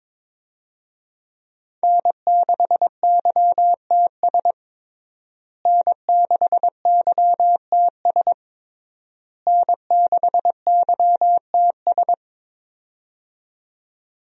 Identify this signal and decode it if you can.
Morse code, heard as a keyed tone, about 22 wpm, 700 Hz. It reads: N6YTS N6YTS N6YTS
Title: USA Call Signs - 5 Characters - Repeated 3x - 22wpm